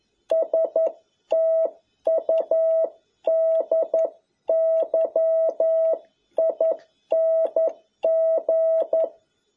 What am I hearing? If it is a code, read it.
Morse code, heard as a keyed tone, about 11 wpm, 650 Hz. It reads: STUDYING
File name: jh8cbh / 2021cw